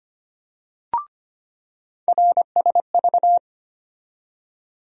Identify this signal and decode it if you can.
Morse code, heard as a keyed tone, about 25 wpm, 700 Hz. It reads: RSV
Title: USA Call Sign Postfixes - 3 Letters 25wpm